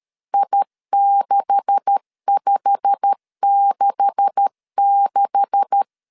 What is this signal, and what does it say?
Morse code, heard as a keyed tone, about 13 wpm, 800 Hz. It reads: I6566
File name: LEZIONI / 9 286 12wpm 2305